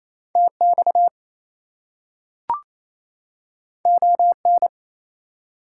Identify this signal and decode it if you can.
Morse code, heard as a keyed tone, about 28 wpm, 700 Hz. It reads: TX ON
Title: US States and Canadian Provinces 28wpm